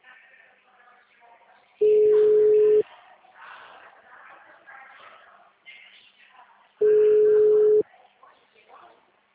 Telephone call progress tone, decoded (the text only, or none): ringback tone